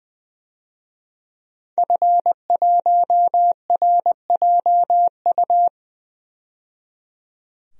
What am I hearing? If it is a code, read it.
Morse code, heard as a keyed tone, about 20 wpm, 700 Hz. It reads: F1RJU
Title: DXCC-100 Call Signs - 5 Characters - 20wpm